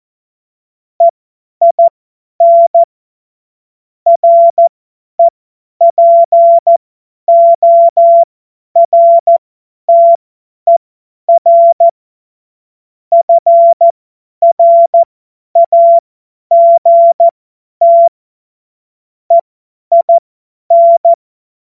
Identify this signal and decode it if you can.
Morse code, heard as a keyed tone, about 14 wpm, spaced as for 10 wpm, 680 Hz. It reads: EIN REPORTER FRAGT EIN
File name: Morsekurs / Witze / Witz 26. Oktober 2021, 50 F70